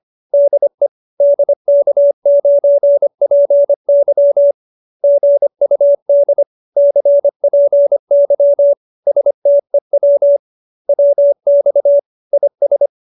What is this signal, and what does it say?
Morse code, heard as a keyed tone, about 25 wpm, 570 Hz. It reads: DE DK9PY GUD CPY STEW WX IS